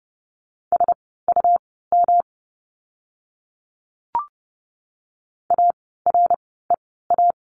Morse code, heard as a keyed tone, about 30 wpm, spaced as for 17 wpm, 700 Hz. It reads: SUM AREA
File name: course / 30:10wpm / 108 Lesson 18f - Letter m - Words 30wpm